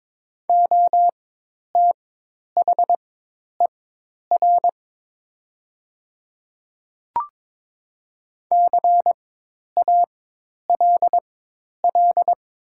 Morse code, heard as a keyed tone, about 22 wpm, spaced as for 10 wpm, 700 Hz. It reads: OTHER CALL
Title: 4x Character Spacing - Top 200 Words - 22wpm